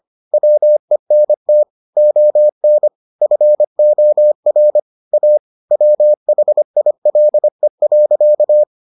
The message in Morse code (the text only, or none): WENT ON FOR A WHILE.